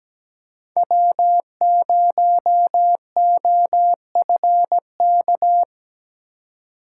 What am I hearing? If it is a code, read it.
Morse code, heard as a keyed tone, about 17 wpm, 700 Hz. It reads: W0OFK
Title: USA Call Signs - 5 Characters - 2x Character Spacing - 17wpm